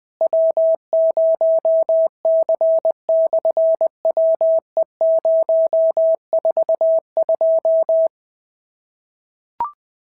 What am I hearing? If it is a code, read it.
Morse code, heard as a keyed tone, about 20 wpm, 650 Hz. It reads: W0C/WE042
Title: SOTA QSO Practice 20wpm